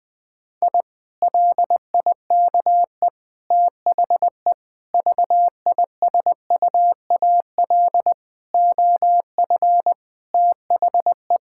Morse code, heard as a keyed tone, about 19 wpm, 700 Hz. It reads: I LIKE THE VISUAL OF THE
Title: Sentences from Top 1000 Words - Review Part 6 - 20wpm